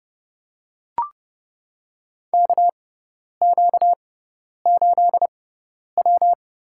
Morse code, heard as a keyed tone, about 31 wpm, spaced as for 10 wpm, 700 Hz. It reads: KQ8W
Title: USA Call Signs - 4 Characters - 6x Character Spacing - 30wpm